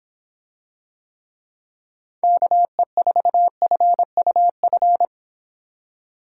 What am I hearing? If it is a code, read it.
Morse code, heard as a keyed tone, about 26 wpm, 700 Hz. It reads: KE4FUF